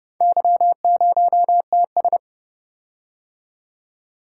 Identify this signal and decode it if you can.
Morse code, heard as a keyed tone, about 30 wpm, 700 Hz. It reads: Y0TS